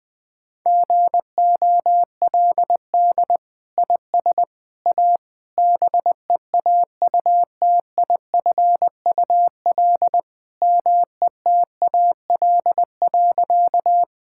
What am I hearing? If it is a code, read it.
Morse code, heard as a keyed tone, about 21 wpm, 700 Hz. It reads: GOLD IS A BEAUTIFUL METAL.
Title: Sentences from Top 1000 Words - Review Part 6 - 20wpm